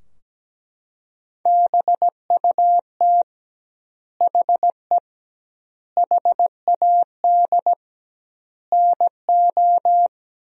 Morse code, heard as a keyed tone, about 17 wpm, 700 Hz. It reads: BUT HE HAD NO
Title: 2x Word-Spacing - Sentences from Top 100 Words - 17wpm